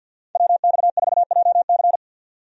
Morse code, heard as a keyed tone, about 47 wpm, 700 Hz. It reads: WX4JC
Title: USA Call Signs - 5 Characters - 50wpm